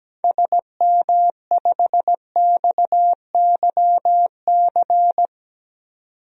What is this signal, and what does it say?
Morse code, heard as a keyed tone, about 17 wpm, 700 Hz. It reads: SM5XYC
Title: DXCC-100 Call Signs - 6 Characters - 17wpm